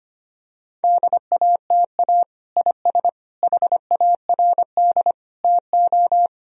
Morse code, heard as a keyed tone, about 25 wpm, 700 Hz. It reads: DATA IS HARD TO
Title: Sentences from Top 1000 Words - Review Part 3 - 25wpm